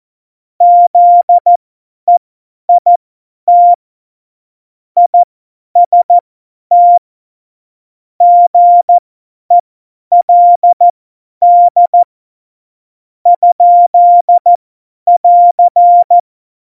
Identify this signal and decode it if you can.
Morse code, heard as a keyed tone, about 14 wpm, spaced as for 9 wpm, 700 Hz. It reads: ZEIT IST GELD? <AR>
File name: Morsekurs / Witze / Witz 04. Februar 2020, 50 F70